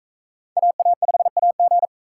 Morse code, heard as a keyed tone, about 44 wpm, 700 Hz. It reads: AA5AG